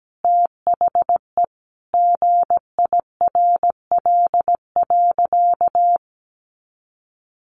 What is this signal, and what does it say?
Morse code, heard as a keyed tone, about 17 wpm, 700 Hz. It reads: THE GIRL.
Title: Sentences from Top 400 Words 17wpm